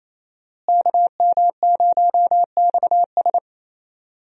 Morse code, heard as a keyed tone, about 28 wpm, 700 Hz. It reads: KM0XS
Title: USA Call Signs - 5 Characters - Repeated 2x - 28wpm